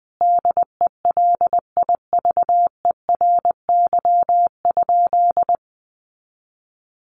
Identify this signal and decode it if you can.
Morse code, heard as a keyed tone, about 20 wpm, 700 Hz. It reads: DELIVERY?
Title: Sentences from Top 1500 Words 20wpm